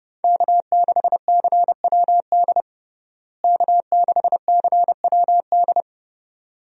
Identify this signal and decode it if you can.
Morse code, heard as a keyed tone, about 32 wpm, 700 Hz. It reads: K6CWD K6CWD
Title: USA Call Signs - 5 Characters - Repeated 2x - 30wpm